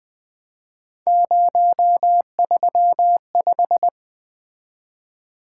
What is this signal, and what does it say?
Morse code, heard as a keyed tone, about 20 wpm, 700 Hz. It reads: 035
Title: Three Letter-Number Combinations - 20wpm